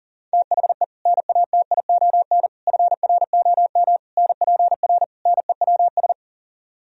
Morse code, heard as a keyed tone, about 52 wpm, 700 Hz. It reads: THE NATION FROM NPR NEWS